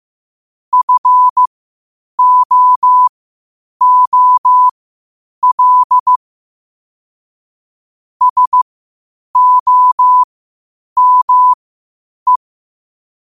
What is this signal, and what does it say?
Morse code, heard as a keyed tone, about 15 wpm, spaced as for 8 wpm, 1000 Hz. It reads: FOOL SOME